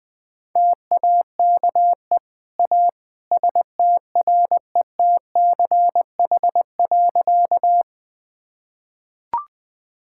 Morse code, heard as a keyed tone, about 20 wpm, 700 Hz. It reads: TAKE A STRETCH.
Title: Easy Sentences with Hints - 20wpm